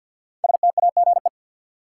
Morse code, heard as a keyed tone, about 53 wpm, 700 Hz. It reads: STAGE
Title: Five Letter Words - 50wpm